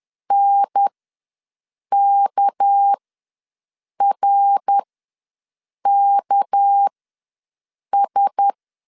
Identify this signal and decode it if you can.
Morse code, heard as a keyed tone, about 10 wpm, spaced as for 6 wpm, 800 Hz. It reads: NKRKS
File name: LEZIONI / 6 PKX0 / 6 PK ws 10wpm 3745